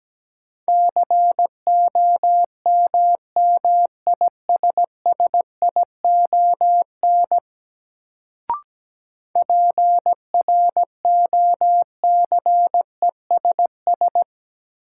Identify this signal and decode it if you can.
Morse code, heard as a keyed tone, about 17 wpm, 700 Hz. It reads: COMMISSION PROCESS
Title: Sentences from Top 1400 Words - New Word Drill Alt - 17wpm